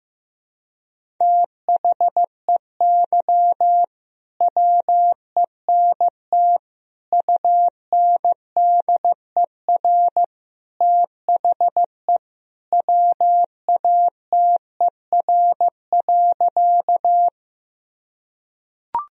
Morse code, heard as a keyed tone, about 16 wpm, 700 Hz. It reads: THEY WENT UNDER THE WATER.
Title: Sentences from Top 1000 Words - Review Part 1 - 15wpm